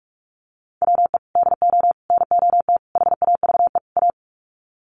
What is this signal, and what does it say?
Morse code, heard as a keyed tone, about 44 wpm, 700 Hz. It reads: WE DO NOT HAVE A
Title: Sets of 5 Words 45wpm